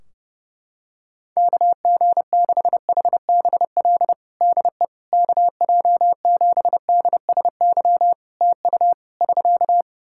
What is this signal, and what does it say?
Morse code, heard as a keyed tone, about 30 wpm, 700 Hz. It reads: KG6HBL DE KJ7DSY TU <SK>
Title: K1USN SST QSO Practice 30wpm